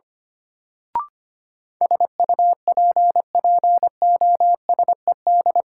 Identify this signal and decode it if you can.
Morse code, heard as a keyed tone, about 25 wpm, 700 Hz. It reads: SUPPOSED
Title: Sets of 3 Words - New Word Drill - 25wpm